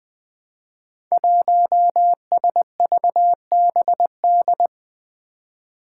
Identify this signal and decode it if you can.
Morse code, heard as a keyed tone, about 20 wpm, 700 Hz. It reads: 1SVBD
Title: Five Challenging Letter-Number Combinations 20wpm